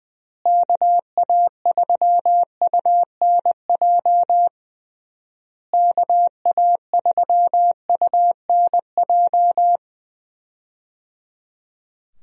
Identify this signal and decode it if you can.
Morse code, heard as a keyed tone, about 20 wpm, 700 Hz. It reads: KA3UNJ KA3UNJ
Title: USA Call Signs - 6 Characters - Repeated 3x - 20wpm